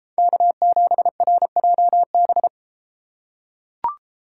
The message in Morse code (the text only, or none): K7RJB